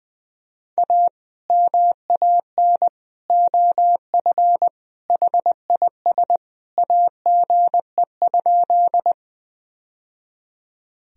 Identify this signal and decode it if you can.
Morse code, heard as a keyed tone, about 20 wpm, 700 Hz. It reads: A MAN OF HIS AGE?
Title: Sentences from Top 1000 Words - Review Part 3 - 20wpm